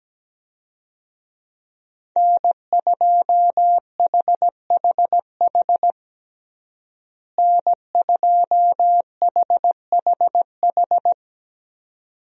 Morse code, heard as a keyed tone, about 17 wpm, 700 Hz. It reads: N2HHH N2HHH
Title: USA Call Signs - 5 Characters - Repeated 2x - 17wpm